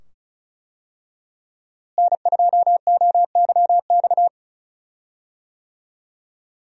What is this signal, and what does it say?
Morse code, heard as a keyed tone, about 36 wpm, 700 Hz. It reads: N2OYX
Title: USA Call Signs - 5 Characters - Repeated 2x - 35wpm